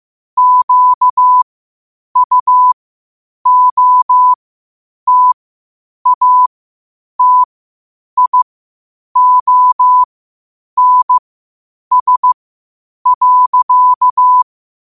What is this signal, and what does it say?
Morse code, heard as a keyed tone, about 15 wpm, spaced as for 8 wpm, 1000 Hz. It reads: QUOTATIONS.
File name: qotdfiles / 8wpm